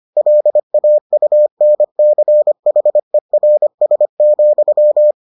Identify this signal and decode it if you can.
Morse code, heard as a keyed tone, about 25 wpm, 600 Hz. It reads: LAUNCHERS,